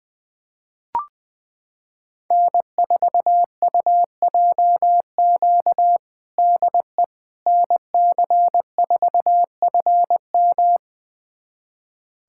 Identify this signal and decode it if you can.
Morse code, heard as a keyed tone, about 20 wpm, 700 Hz. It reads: N4UJQ DE NC4FM
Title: USA Callsign DE Callsign - 5 Characters - 20wpm